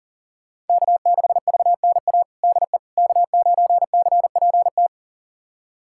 Morse code, heard as a keyed tone, about 37 wpm, 700 Hz. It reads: K6VNA DE K9CPT
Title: USA Callsign DE Callsign - 5 Characters - 40wpm